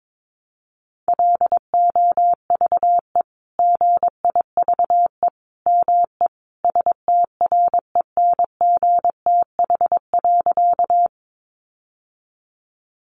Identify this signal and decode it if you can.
Morse code, heard as a keyed tone, about 22 wpm, 700 Hz. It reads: LOVE GIVE ME STRENGTH.
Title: Sentences from Top 1300 Words 22wpm